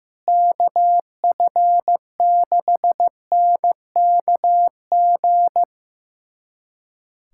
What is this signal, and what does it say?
Morse code, heard as a keyed tone, about 15 wpm, 700 Hz. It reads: KF6NKG
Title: USA Callsign DE Callsign - 6 Characters - 15wpm